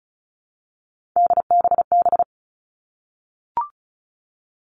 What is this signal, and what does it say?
Morse code, heard as a keyed tone, about 34 wpm, 700 Hz. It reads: DBB
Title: All Cut Numbers - 3-Digit 35wpm